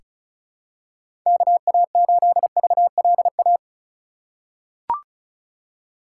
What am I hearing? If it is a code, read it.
Morse code, heard as a keyed tone, about 34 wpm, 700 Hz. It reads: KA8VLA
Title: USA Call Signs - 6 Characters - 4x Character Spacing - 35wpm